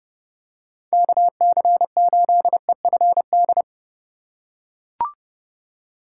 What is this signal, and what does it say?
Morse code, heard as a keyed tone, about 29 wpm, 700 Hz. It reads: KC8EFD